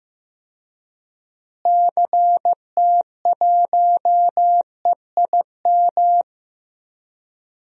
Morse code, heard as a keyed tone, about 15 wpm, 700 Hz. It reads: CT1EIM
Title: DXCC-100 Call Signs - 6 Characters - Repeated 3x - 15wpm